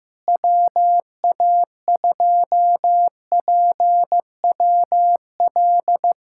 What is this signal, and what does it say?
Morse code, heard as a keyed tone, about 15 wpm, 700 Hz. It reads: WA2PWL
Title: USA Callsign DE Callsign 15wpm